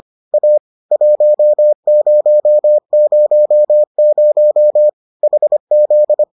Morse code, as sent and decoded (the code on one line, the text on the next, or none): .- / .---- ----- ----- ----- / .... --..
A 1000 HZ